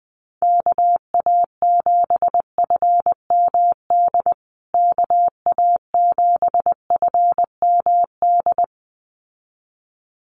Morse code, heard as a keyed tone, about 20 wpm, 700 Hz. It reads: KA7FMD KA7FMD
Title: USA Call Signs Repeated 2x 20wpm